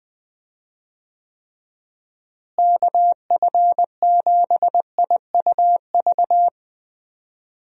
Morse code, heard as a keyed tone, about 20 wpm, 700 Hz. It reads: KF7IUV
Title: USA Call Signs - 6 Characters - Repeated 2x - 20wpm